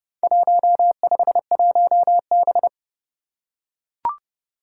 Morse code, heard as a keyed tone, about 30 wpm, 700 Hz. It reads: 151B
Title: Four Challenging Letter-Number Combinations 30wpm